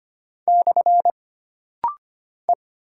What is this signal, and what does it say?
Morse code, heard as a keyed tone, about 25 wpm, 700 Hz. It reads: / E
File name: Single Character (Letter:Number:Punctuation:Prosign) - Rapid-Fire - 25wpm